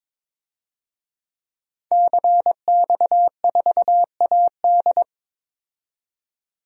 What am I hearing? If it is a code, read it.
Morse code, heard as a keyed tone, about 22 wpm, 700 Hz. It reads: CX4AD
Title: DXCC-100 Call Signs - 5 Characters - 22wpm